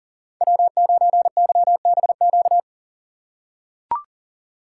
Morse code, heard as a keyed tone, about 37 wpm, 700 Hz. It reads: W9YBQ
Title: USA Call Signs - 2x Character Spacing - 40wpm